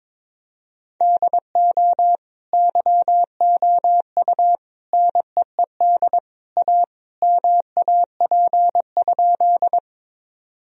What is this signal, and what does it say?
Morse code, heard as a keyed tone, about 21 wpm, 700 Hz. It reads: DO YOU NEED A MAP?
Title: Sentences from Top 1000 Words - Review Part 2 - 22wpm